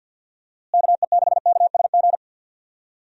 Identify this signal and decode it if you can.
Morse code, heard as a keyed tone, about 47 wpm, 700 Hz. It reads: KE6KSG
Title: USA Call Signs - 6 Characters - Repeated 2x - 50wpm